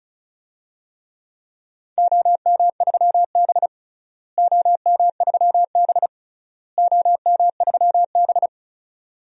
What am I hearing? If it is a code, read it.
Morse code, heard as a keyed tone, about 34 wpm, 700 Hz. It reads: OM3B OM3B OM3B
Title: DXCC-100 Call Signs - 4 Characters - Repeated 3x - 35wpm